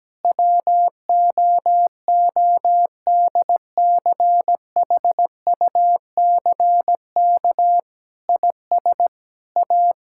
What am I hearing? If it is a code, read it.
Morse code, heard as a keyed tone, about 17 wpm, 700 Hz. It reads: WOODCHUCK IS A
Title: Pangram Challenge v2 - 17wpm